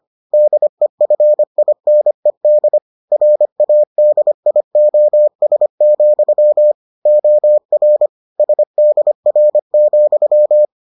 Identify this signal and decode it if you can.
Morse code, heard as a keyed tone, about 25 wpm, 600 Hz. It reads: DEFINED RADIOS, OR SDR,